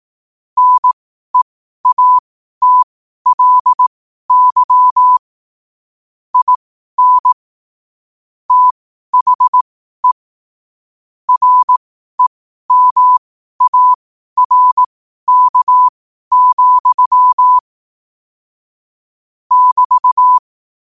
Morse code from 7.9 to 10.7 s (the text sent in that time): THE